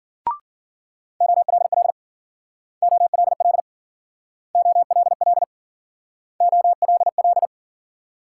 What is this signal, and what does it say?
Morse code, heard as a keyed tone, about 55 wpm, 700 Hz. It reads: OLL OLL OLL OLL